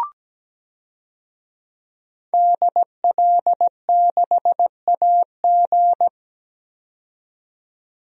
Morse code, heard as a keyed tone, about 17 wpm, 700 Hz. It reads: DL6AG
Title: DXCC-100 Call Signs - 5 Characters - 17wpm